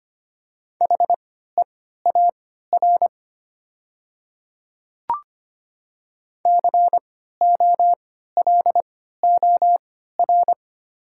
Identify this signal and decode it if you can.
Morse code, heard as a keyed tone, about 25 wpm, spaced as for 14 wpm, 700 Hz. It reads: HEAR COLOR
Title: 3x Character Spacing - Top 300 Words - 25wpm